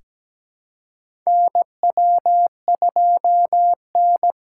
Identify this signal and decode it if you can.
Morse code, heard as a keyed tone, about 17 wpm, 700 Hz. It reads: NW2N
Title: USA Call Signs - 4 Characters - 4x Character Spacing - 17wpm